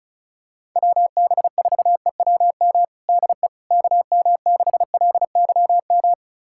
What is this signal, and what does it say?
Morse code, heard as a keyed tone, about 36 wpm, 700 Hz. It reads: WB4EWM DE KM6LYM
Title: USA Callsign DE Callsign 35wpm